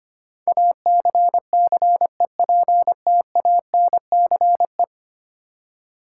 Morse code, heard as a keyed tone, about 25 wpm, 700 Hz. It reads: ACCEPTANCE
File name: Ten Letter Words - 25wpm